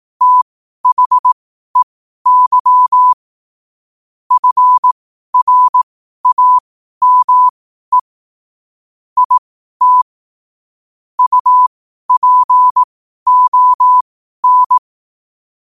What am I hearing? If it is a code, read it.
Morse code, heard as a keyed tone, about 18 wpm, spaced as for 12 wpm, 1000 Hz. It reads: THEY FRAME IT UPON
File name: qotdfiles / 12wpm